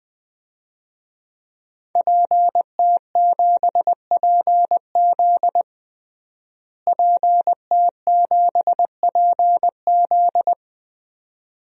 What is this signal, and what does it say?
Morse code, heard as a keyed tone, about 21 wpm, 700 Hz. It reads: PT7PZ PT7PZ
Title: DXCC-100 Call Signs - 5 Characters - Repeated 2x - 20wpm